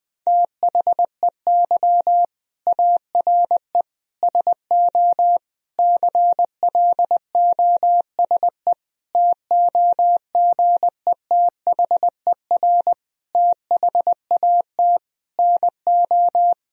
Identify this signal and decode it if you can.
Morse code, heard as a keyed tone, about 20 wpm, 700 Hz. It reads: THEY ARE SO CLOSE TOGETHER THAT NO
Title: Sentences from Top 1000 Words - Review Part 1 - 20wpm